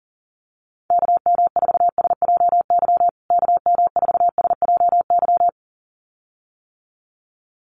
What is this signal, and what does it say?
Morse code, heard as a keyed tone, about 40 wpm, 700 Hz. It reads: KM4SJY KM4SJY
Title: USA Call Signs Repeated 2x 40wpm